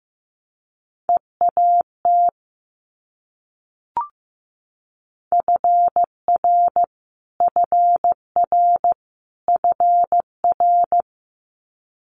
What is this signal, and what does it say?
Morse code, heard as a keyed tone, about 15 wpm, 700 Hz. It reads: EAT FR FR FR
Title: Multi-letter Phonemes Repeated 3x 15wpm